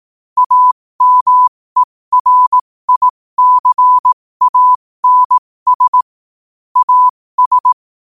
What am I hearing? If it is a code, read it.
Morse code, heard as a keyed tone, about 18 wpm, spaced as for 15 wpm, 1000 Hz. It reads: AMERICANS AS
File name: qotdfiles / 15wpm